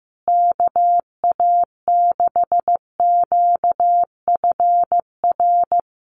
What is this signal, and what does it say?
Morse code, heard as a keyed tone, about 15 wpm, 700 Hz. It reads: KA6QFR
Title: USA Call Signs Repeated 3x 15wpm